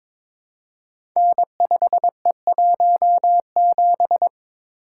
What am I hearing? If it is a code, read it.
Morse code, heard as a keyed tone, about 22 wpm, 700 Hz. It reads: N5E17
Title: Code Groups - 5 Letter:Numbers - 22wpm